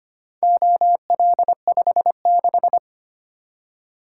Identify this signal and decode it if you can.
Morse code, heard as a keyed tone, about 25 wpm, 700 Hz. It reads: OL56
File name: practice / 25wpm / Four Letter-Number Combinations - 25wpm